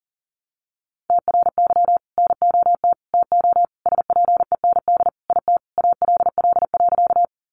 Morse code, heard as a keyed tone, about 40 wpm, 700 Hz. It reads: TRY NOT TO SPEND IT ALL.
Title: Sentences from Top 800 Words 40wpm